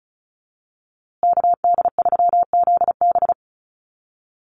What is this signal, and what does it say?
Morse code, heard as a keyed tone, about 34 wpm, 700 Hz. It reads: KD3ZB
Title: USA Call Signs Repeated 2x 35wpm